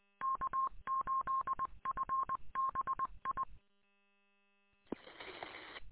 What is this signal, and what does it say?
Morse code, heard as a keyed tone, about 21 wpm, 1070 Hz. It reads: K8FBI